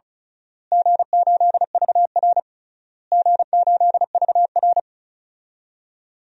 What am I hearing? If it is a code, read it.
Morse code, heard as a keyed tone, about 36 wpm, 700 Hz. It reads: G8VR G8VR
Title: DXCC-100 Call Signs - 4 Characters - Repeated 3x - 35wpm